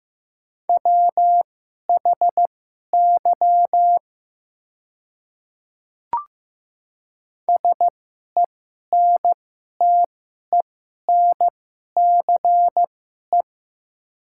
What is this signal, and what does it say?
Morse code, heard as a keyed tone, about 15 wpm, spaced as for 11 wpm, 700 Hz. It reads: WHY SENTENCE
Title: 2x Character Spacing - Top 200 Words - 15wpm